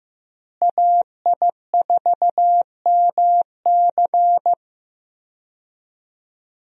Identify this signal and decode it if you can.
Morse code, heard as a keyed tone, about 15 wpm, 700 Hz. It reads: AI4MC
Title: USA Call Signs - 5 Characters - Repeated 3x - 15wpm